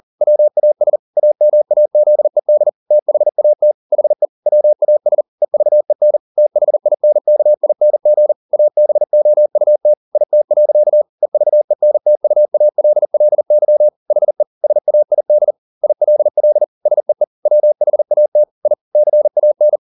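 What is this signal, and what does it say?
Morse code, heard as a keyed tone, about 40 wpm, 600 Hz. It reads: WAS AMAZED THAT HE WAS EVEN THINKING ABOUT IT. EVENTUALLY HE SAID ILL SEE WHAT I CAN
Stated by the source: Morse code at 40 wpm, practice text, CW med Gnister.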